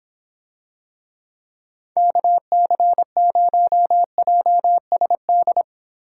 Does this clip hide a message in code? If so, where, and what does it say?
Yes: Morse code from 2.0 to 5.6 s, heard as a keyed tone, about 26 wpm, 700 Hz. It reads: KC0JSD